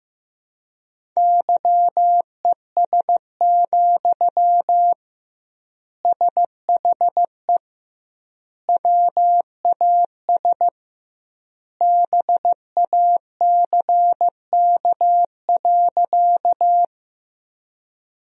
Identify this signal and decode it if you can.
Morse code, heard as a keyed tone, about 15 wpm, 700 Hz. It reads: YES, SHE WAS BACK.
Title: 2x Word-Spacing - Sentences from Top 500 Words - 15wpm